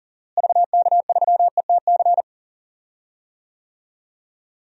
Morse code, heard as a keyed tone, about 37 wpm, 700 Hz. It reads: VK3ETC
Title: DXCC-100 Call Signs - 6 Characters - Repeated 3x - 40wpm